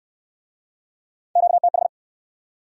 Morse code, heard as a keyed tone, about 65 wpm, 700 Hz. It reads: QTH